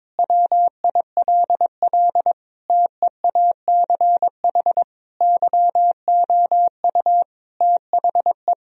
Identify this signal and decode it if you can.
Morse code, heard as a keyed tone, about 22 wpm, 700 Hz. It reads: WILL TEACH YOU THE
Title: Sentences from Top 1000 Words - Review Part 2 - 22wpm